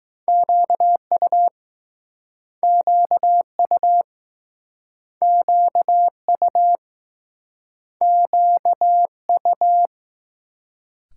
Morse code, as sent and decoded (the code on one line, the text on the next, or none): --.- ..- / --.- ..- / --.- ..- / --.- ..-
QU QU QU QU